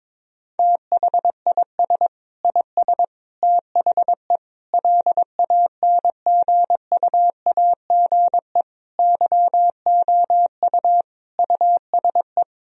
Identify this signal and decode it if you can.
Morse code, heard as a keyed tone, about 22 wpm, 700 Hz. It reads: THIS IS THE LANGUAGE YOU USE
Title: Sentences from Top 1000 Words - Review Part 3 - 22wpm